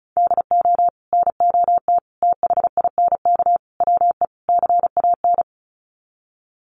Morse code, heard as a keyed tone, about 34 wpm, 700 Hz. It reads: DO NOT THINK WE CAN
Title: Sets of 5 Words Encore2 35wpm